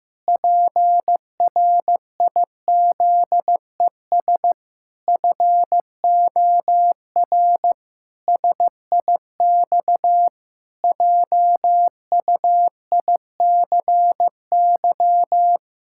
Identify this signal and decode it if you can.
Morse code, heard as a keyed tone, about 15 wpm, 700 Hz. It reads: PRIZES FOR SIX JUICY
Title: Pangram Challenge v2 - 15wpm